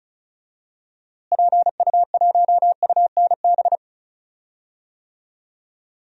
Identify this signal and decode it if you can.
Morse code, heard as a keyed tone, about 36 wpm, 700 Hz. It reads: PU1UNB
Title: DXCC-100 Call Signs - 6 Characters - Repeated 2x - 35wpm